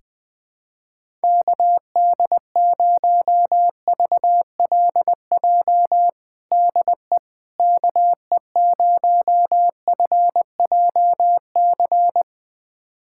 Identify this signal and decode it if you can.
Morse code, heard as a keyed tone, about 20 wpm, 700 Hz. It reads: KD0VLJ DE KE0FJC
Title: USA Callsign DE Callsign - 6 Characters - 20wpm